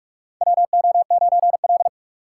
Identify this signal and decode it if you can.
Morse code, heard as a keyed tone, about 46 wpm, 700 Hz. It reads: WO9L
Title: USA Call Signs - 4 Characters - 3x Character Spacing - 45wpm